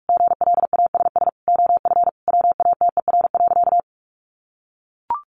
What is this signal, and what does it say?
Morse code, heard as a keyed tone, about 44 wpm, 700 Hz. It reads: GLASS OF WATER.
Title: Sentences from Top 1300 Words 45wpm